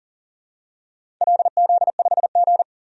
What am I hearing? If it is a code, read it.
Morse code, heard as a keyed tone, about 40 wpm, 700 Hz. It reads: LZ5G
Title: DXCC-100 Call Signs - 4 Characters - 40wpm